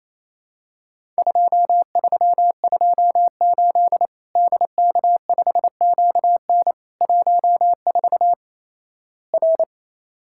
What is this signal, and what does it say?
Morse code, heard as a keyed tone, about 27 wpm, 700 Hz. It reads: 2328 DK5QN 14 R
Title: Worked All Europe DX Contest - QTC Exchange - 28wpm